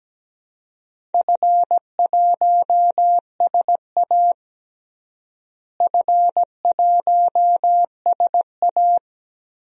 Morse code, heard as a keyed tone, about 17 wpm, 700 Hz. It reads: F1SA F1SA
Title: DXCC-100 Call Signs - 4 Characters - Repeated 3x - 17wpm